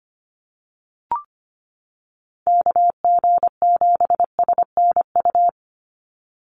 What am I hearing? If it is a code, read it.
Morse code, heard as a keyed tone, about 25 wpm, 700 Hz. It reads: KG7SNU
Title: USA Call Signs 25wpm